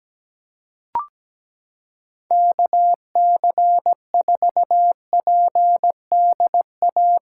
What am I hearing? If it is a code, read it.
Morse code, heard as a keyed tone, about 17 wpm, 700 Hz. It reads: KC4PDA
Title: USA Callsign DE Callsign - 6 Characters - 17wpm